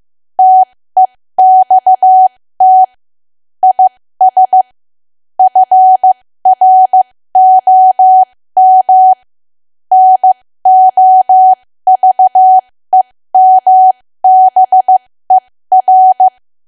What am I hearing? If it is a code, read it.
Morse code, heard as a keyed tone, about 15 wpm, spaced as for 13 wpm, 750 Hz. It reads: TEXT IS FROM NOVEMBER
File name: cwa / W1AW-Level2 / 13WPM